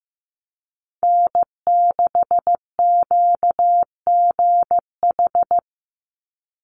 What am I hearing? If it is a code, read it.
Morse code, heard as a keyed tone, about 15 wpm, 700 Hz. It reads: N6QGH
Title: USA Call Signs 15wpm